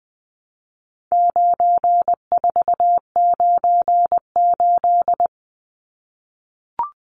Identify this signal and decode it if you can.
Morse code, heard as a keyed tone, about 20 wpm, 700 Hz. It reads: 9498
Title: Numbers - 4-Digit 20wpm